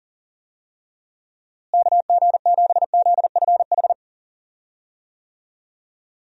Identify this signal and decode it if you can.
Morse code, heard as a keyed tone, about 40 wpm, 700 Hz. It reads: KG7ZFH
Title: USA Call Signs - 6 Characters - Repeated 2x - 40wpm